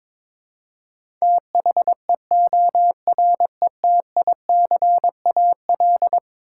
Morse code, heard as a keyed tone, about 22 wpm, 700 Hz. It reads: THEORETICAL
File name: practice / 22wpm / Eleven Letter Words - 22wpm